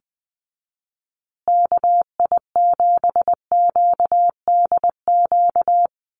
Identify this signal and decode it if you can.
Morse code, heard as a keyed tone, about 20 wpm, 700 Hz. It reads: KI7QDQ
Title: USA Call Signs Repeated 2x 20wpm